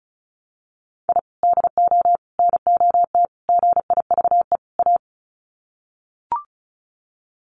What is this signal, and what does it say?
Morse code, heard as a keyed tone, about 34 wpm, 700 Hz. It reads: I DO NOT GIVE A
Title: Sets of 5 Words Encore2 35wpm